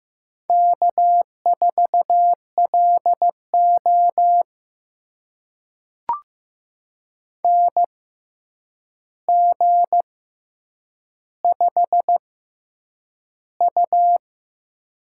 Morse code, heard as a keyed tone, about 15 wpm, spaced as for 8 wpm, 700 Hz. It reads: K4LO NG5U
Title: USA Call Signs - 4 Characters - 6x Character Spacing - 15wpm